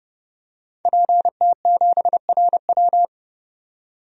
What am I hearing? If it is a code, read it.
Morse code, heard as a keyed tone, about 30 wpm, 700 Hz. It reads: PT7RW
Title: DXCC-100 Call Signs - 5 Characters - 30wpm